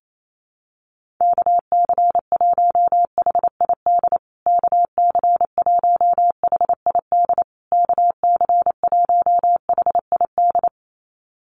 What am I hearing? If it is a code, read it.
Morse code, heard as a keyed tone, about 28 wpm, 700 Hz. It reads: KC1HID KC1HID KC1HID
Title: USA Call Signs Repeated 3x 28wpm